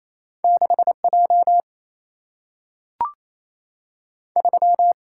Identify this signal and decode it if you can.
Morse code, heard as a keyed tone, about 28 wpm, 700 Hz. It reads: 6 J 3
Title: Single-Letter Single-Number 28wpm